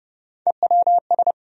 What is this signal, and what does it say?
Morse code, heard as a keyed tone, about 30 wpm, 700 Hz. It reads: EWS
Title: USA Call Sign Postfixes - 3 Letters 30wpm